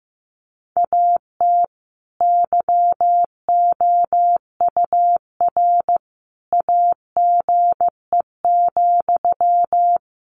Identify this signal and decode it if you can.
Morse code, heard as a keyed tone, about 15 wpm, 700 Hz. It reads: AT YOUR AGE,
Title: Sentences from Top 1500 Words 15wpm